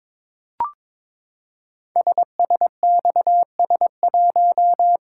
Morse code, heard as a keyed tone, about 22 wpm, 700 Hz. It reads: SSXS1
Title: Code Groups - 5 Letter:Numbers - 22wpm